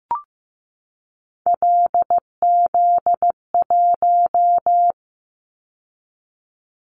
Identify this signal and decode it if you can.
Morse code, heard as a keyed tone, about 15 wpm, 700 Hz. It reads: LZ1
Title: Callsign Prefixes for Top 100 DXCC - 15wpm